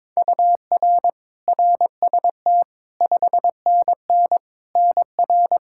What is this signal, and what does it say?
Morse code, heard as a keyed tone, about 22 wpm, 700 Hz. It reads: UR RST 5NN NR
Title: K1USN SST QSO Practice 22wpm